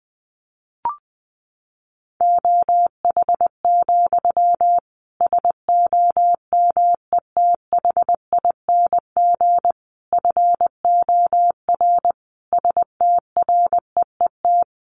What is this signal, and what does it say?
Morse code, heard as a keyed tone, about 20 wpm, 700 Hz. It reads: OH, SOMETHING FOR STREET
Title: Sentences from Top 1500 Words 20wpm